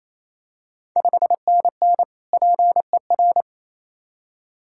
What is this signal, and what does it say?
Morse code, heard as a keyed tone, about 27 wpm, 700 Hz. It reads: 5NN PER
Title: Ontario QSO Party Exchange 28wpm